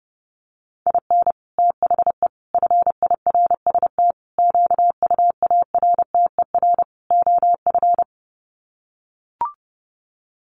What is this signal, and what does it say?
Morse code, heard as a keyed tone, about 30 wpm, 700 Hz. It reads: IN THE FIRST QUARTER OF
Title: Sets of 5 Words Encore2 30wpm